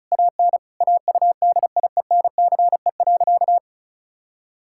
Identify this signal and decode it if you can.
Morse code, heard as a keyed tone, about 36 wpm, 700 Hz. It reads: AN AUDIENCE.
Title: Sentences from Top 1000 Words - Review Part 5 - 35wpm